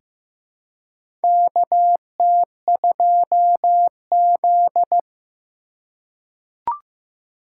Morse code, heard as a keyed tone, about 15 wpm, 700 Hz. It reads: KT2Z
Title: USA Call Signs - 4 Characters - 6x Character Spacing - 15wpm